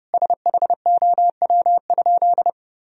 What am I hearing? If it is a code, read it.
Morse code, heard as a keyed tone, about 30 wpm, 700 Hz. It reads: SHOW?
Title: Sentences from Top 1000 Words - Review Part 2 - 30wpm